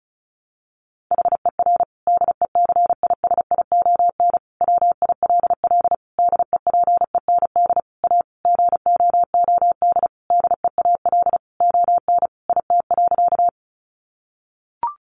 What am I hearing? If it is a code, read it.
Morse code, heard as a keyed tone, about 34 wpm, 700 Hz. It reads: HER DECISION WILL DEPEND A GOOD DEAL ON IT.
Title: Sentences from Top 1300 Words 35wpm